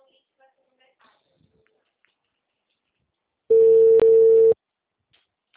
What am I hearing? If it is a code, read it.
Ringback tone (China, 450 Hz), with speech.